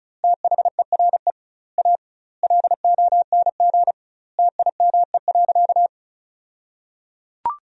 Morse code, heard as a keyed tone, about 36 wpm, 700 Hz. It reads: THERE A LONG TIME.
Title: 2x Word-Spacing - Sentences from Top 400 Words - 35wpm